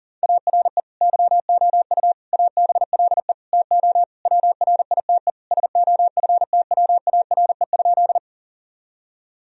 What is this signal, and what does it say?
Morse code, heard as a keyed tone, about 37 wpm, 700 Hz. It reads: ARE YOU ABLE TO WRITE SOFTWARE?
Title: Sentences from Top 1000 Words - Review Part 3 - 40wpm